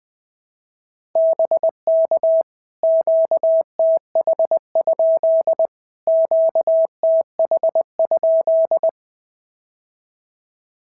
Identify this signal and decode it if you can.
Morse code, heard as a keyed tone, about 19 wpm, 650 Hz. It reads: BK QTH? QTH?
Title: POTA QSO Practice - 20wpm